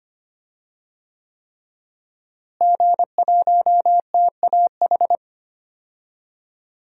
Morse code, heard as a keyed tone, about 25 wpm, 700 Hz. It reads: G1TAH
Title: DXCC-100 Call Signs - 5 Characters - Repeated 3x - 25wpm